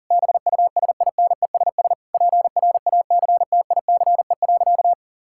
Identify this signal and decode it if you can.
Morse code, heard as a keyed tone, about 40 wpm, 700 Hz. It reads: BUSINESS PRACTICE.
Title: Sentences from Top 1000 Words - Review Part 6 - 40wpm